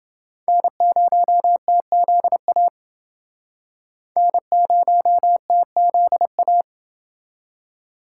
Morse code, heard as a keyed tone, about 29 wpm, 700 Hz. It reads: N0TZA N0TZA